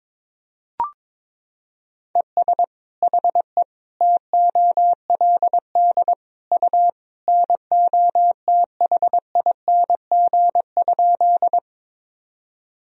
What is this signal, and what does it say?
Morse code, heard as a keyed tone, about 22 wpm, 700 Hz. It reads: ES HE TOLD U NOTHING?
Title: Sentences from Top 500 Words and Abbreviations 22wpm